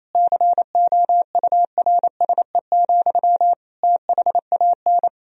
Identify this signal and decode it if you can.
Morse code, heard as a keyed tone, about 28 wpm, 700 Hz. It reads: COURSE, THAN
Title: Sentences from Top 1000 Words - Review Part 3 - 28wpm